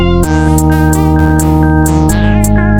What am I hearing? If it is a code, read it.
There is no code.